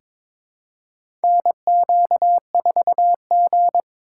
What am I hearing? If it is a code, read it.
Morse code, heard as a keyed tone, about 22 wpm, 700 Hz. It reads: NQ4G
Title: USA Call Signs - 2x Character Spacing - 22wpm